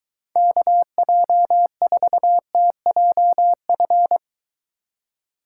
Morse code, heard as a keyed tone, about 23 wpm, 700 Hz. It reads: KJ4TJF